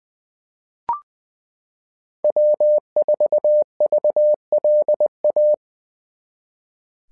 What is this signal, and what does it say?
Morse code, heard as a keyed tone, about 20 wpm, 600 Hz. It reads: W4VLA
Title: SOTA QSO Practice 20wpm